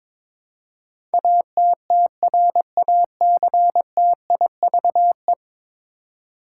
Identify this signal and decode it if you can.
Morse code, heard as a keyed tone, about 22 wpm, 700 Hz. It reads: ATTRACTIVE
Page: morse-code-ninja-practice-22wpm